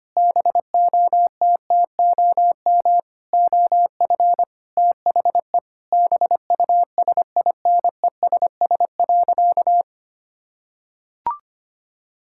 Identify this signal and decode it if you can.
Morse code, heard as a keyed tone, about 25 wpm, 700 Hz. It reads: BOTTOM OF THE BUSINESS.
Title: Sentences from Top 1000 Words - Review Part 6 - 25wpm